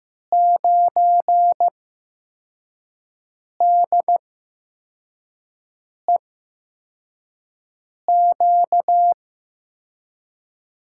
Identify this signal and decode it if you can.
Morse code, heard as a keyed tone, about 15 wpm, spaced as for 4 wpm, 700 Hz. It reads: 9DEQ
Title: USA Call Signs - 5 Characters - 8x Character Spacing - 15wpm